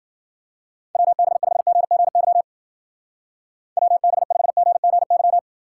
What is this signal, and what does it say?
Morse code, heard as a keyed tone, about 50 wpm, 700 Hz. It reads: WB5GGY WB5GGY
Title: Speed Racing - USA 2x3 Call Signs 40wpm